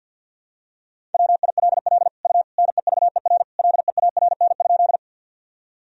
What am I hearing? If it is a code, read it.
Morse code, heard as a keyed tone, about 50 wpm, 700 Hz. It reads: WILL U NEVER LEARN?
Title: Sentences from Top 300 Words and Abbreviations 50wpm